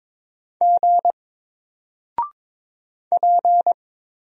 Morse code, heard as a keyed tone, about 22 wpm, 700 Hz. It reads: G P
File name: Single-Letter-Number-22wpm